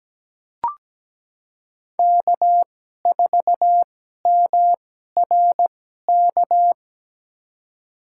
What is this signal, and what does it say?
Morse code, heard as a keyed tone, about 17 wpm, spaced as for 12 wpm, 700 Hz. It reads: K4MRK
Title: USA Call Signs - 5 Characters - 2x Character Spacing - 17wpm